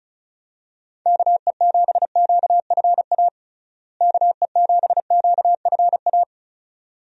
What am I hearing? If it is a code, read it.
Morse code, heard as a keyed tone, about 36 wpm, 700 Hz. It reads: KE7QFA KE7QFA
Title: USA Call Signs - 6 Characters - Repeated 2x - 35wpm